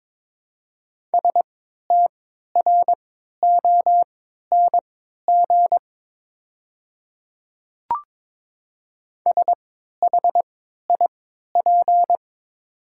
Morse code, heard as a keyed tone, about 22 wpm, spaced as for 12 wpm, 700 Hz. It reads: STRONG SHIP
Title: 3x Character Spacing - Top 500 Words - 22wpm